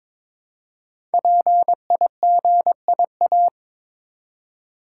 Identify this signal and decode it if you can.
Morse code, heard as a keyed tone, about 22 wpm, 700 Hz. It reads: PIGIA
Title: Code Groups - 5 Letters - 22wpm